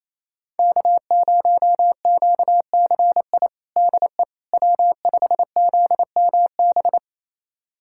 Morse code, heard as a keyed tone, about 28 wpm, 700 Hz. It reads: K0QCI DE W5ZMB
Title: USA Callsign DE Callsign - 5 Characters - 28wpm